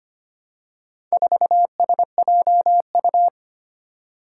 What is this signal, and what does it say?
Morse code, heard as a keyed tone, about 25 wpm, 700 Hz. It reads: 4SJU
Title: Four Challenging Letter-Number Combinations 25wpm